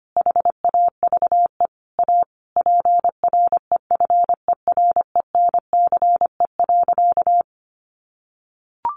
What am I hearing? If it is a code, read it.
Morse code, heard as a keyed tone, about 25 wpm, 700 Hz. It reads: HAVE A PREFERENCE.
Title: Sentences from Top 1500 Words 25wpm